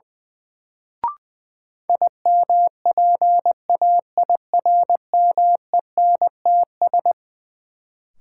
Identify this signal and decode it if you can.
Morse code, heard as a keyed tone, about 21 wpm, 700 Hz. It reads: IMPAIRMENTS
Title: Sets of 5 Words Encore - New Word Drill - 20wpm